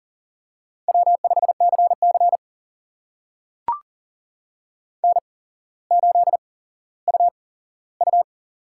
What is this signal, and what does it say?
Morse code, heard as a keyed tone, about 40 wpm, spaced as for 17 wpm, 700 Hz. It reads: W5CC N8UU
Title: USA Call Signs - 8x Character Spacing - 40wpm